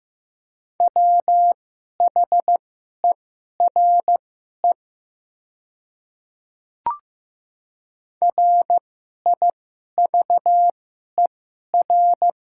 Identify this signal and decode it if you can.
Morse code, heard as a keyed tone, about 15 wpm, spaced as for 11 wpm, 700 Hz. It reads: WHERE RIVER
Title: 2x Character Spacing - Top 300 Words - 15wpm